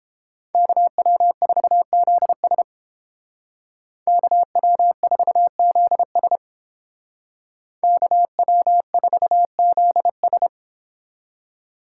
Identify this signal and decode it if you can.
Morse code, heard as a keyed tone, about 30 wpm, 700 Hz. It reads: KW4ZS KW4ZS KW4ZS